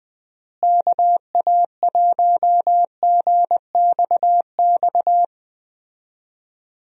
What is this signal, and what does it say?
Morse code, heard as a keyed tone, about 21 wpm, 700 Hz. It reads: KA1GXX
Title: USA Call Signs - 6 Characters - Repeated 2x - 20wpm